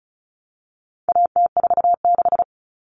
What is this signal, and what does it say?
Morse code, heard as a keyed tone, about 34 wpm, 700 Hz. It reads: AT46
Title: All Cut Numbers - 4-Digit 35wpm